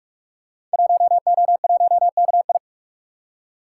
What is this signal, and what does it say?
Morse code, heard as a keyed tone, about 46 wpm, 700 Hz. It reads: 1O1KI
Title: Code Groups - 5 Letter:Numbers - 45wpm